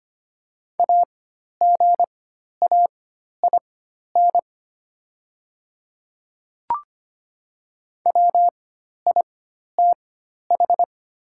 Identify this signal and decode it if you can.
Morse code, heard as a keyed tone, about 25 wpm, spaced as for 12 wpm, 700 Hz. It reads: AGAIN WITH
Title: 4x Character Spacing - Top 300 Words - 25wpm